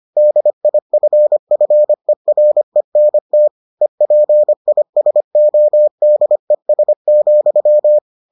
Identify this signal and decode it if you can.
Morse code, heard as a keyed tone, about 25 wpm, 600 Hz. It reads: DIFFERENT EPISODES,